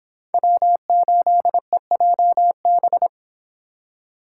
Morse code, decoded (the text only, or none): W8EJB